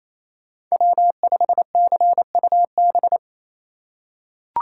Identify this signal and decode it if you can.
Morse code, heard as a keyed tone, about 28 wpm, 700 Hz. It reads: W5CUB
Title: USA Call Signs - 5 Characters - 4x Character Spacing - 28wpm